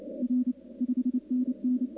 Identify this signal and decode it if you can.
Morse code, heard as a keyed tone, about 29 wpm, 280 Hz. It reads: R 5NN
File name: contests / 2019-cw-wpx-kt5j-k5pi- -2330 0-11